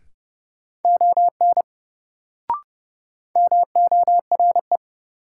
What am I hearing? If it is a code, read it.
Morse code, heard as a keyed tone, about 29 wpm, 700 Hz. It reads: ON MORE
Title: Sentences from Top 100 Words - New Word Drill - 30wpm